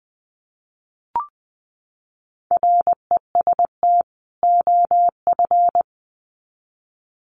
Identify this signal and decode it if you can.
Morse code, heard as a keyed tone, about 20 wpm, 700 Hz. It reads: REST OF
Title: Sets of 2 Words 20wpm